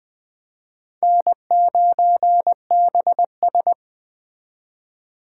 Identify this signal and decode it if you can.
Morse code, heard as a keyed tone, about 21 wpm, 700 Hz. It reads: N9BS